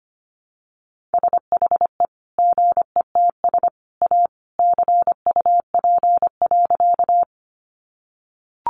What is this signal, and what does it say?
Morse code, heard as a keyed tone, about 25 wpm, 700 Hz. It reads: SHE GETS A CUP.
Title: Additional Easy Phrases 25wpm